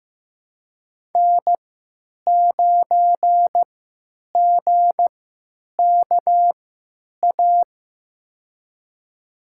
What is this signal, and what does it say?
Morse code, heard as a keyed tone, about 15 wpm, spaced as for 9 wpm, 700 Hz. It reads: N9GKA
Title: USA Call Signs - 5 Characters - 3x Character Spacing - 15wpm